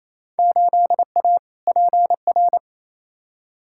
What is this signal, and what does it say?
Morse code, heard as a keyed tone, about 28 wpm, 700 Hz. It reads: 8A PR
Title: Field Day Exchanges Encore - 28wpm